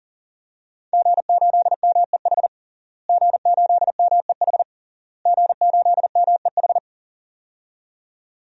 Morse code, heard as a keyed tone, about 40 wpm, 700 Hz. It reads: G8MEH G8MEH G8MEH
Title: DXCC-100 Call Signs - 5 Characters - Repeated 3x - 40wpm